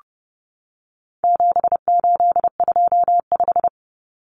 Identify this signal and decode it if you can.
Morse code, heard as a keyed tone, about 30 wpm, 700 Hz. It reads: 7825
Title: Numbers - 4-Digit 30wpm